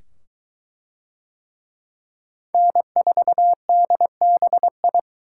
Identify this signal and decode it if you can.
Morse code, heard as a keyed tone, about 22 wpm, 700 Hz. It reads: N4DBI